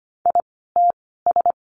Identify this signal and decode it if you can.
Morse code, heard as a keyed tone, about 25 wpm, spaced as for 16 wpm, 700 Hz. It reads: ITS
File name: course / 25:10wpm / 094 Lesson 16f - Letter c - Words 25wpm